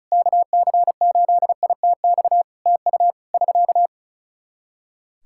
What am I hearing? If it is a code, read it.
Morse code, heard as a keyed tone, about 36 wpm, 700 Hz. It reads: KC8ITX TU <SK>
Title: Practice QSO - Exchange 4 - Ending - 35wpm